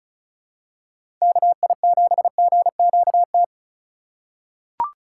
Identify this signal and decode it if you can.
Morse code, heard as a keyed tone, about 34 wpm, 700 Hz. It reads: KI7GQT
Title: USA Call Signs - 6 Characters - 2x Character Spacing - 35wpm